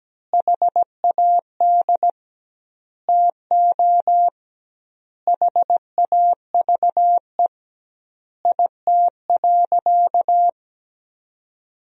Morse code, heard as a keyed tone, about 17 wpm, 700 Hz. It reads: HAD TO HAVE IT.
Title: 2x Word-Spacing - Sentences from Top 100 Words - 17wpm